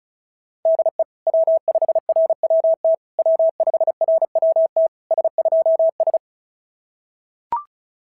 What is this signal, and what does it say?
Morse code, heard as a keyed tone, about 33 wpm, 660 Hz. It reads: DE W5RWT W5RWT S2S
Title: SOTA QSO Practice 35wpm